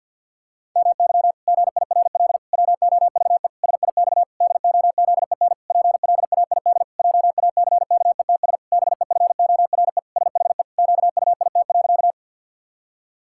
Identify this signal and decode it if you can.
Morse code, heard as a keyed tone, about 52 wpm, 700 Hz. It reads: MY GIRL WOVE SIX DOZEN PLAID JACKETS BEFORE SHE QUIT.
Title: Pangram Challenge v2 - 50wpm